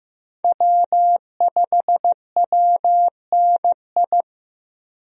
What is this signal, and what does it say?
Morse code, heard as a keyed tone, about 15 wpm, 700 Hz. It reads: W5WNI
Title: USA Call Signs - 5 Characters - 15wpm